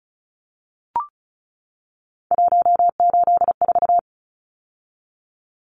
Morse code, heard as a keyed tone, about 34 wpm, 700 Hz. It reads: 184
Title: Numbers - 3-Digit 35wpm